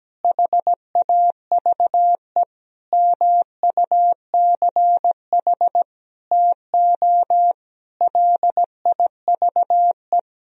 Morse code, heard as a keyed tone, about 17 wpm, 700 Hz. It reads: HAVE MUCH TO LIVE
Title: Sentences from Top 1000 Words - Review Part 3 - 17wpm